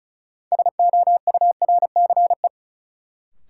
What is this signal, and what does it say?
Morse code, heard as a keyed tone, about 34 wpm, 700 Hz. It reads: SOURCE
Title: Sentences from Top 800 Words - New Word Drill - 35wpm